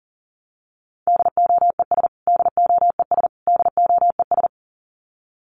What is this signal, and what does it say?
Morse code, heard as a keyed tone, about 40 wpm, 700 Hz. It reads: DOES DOES DOES
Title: Top 300 Words Repeated 40wpm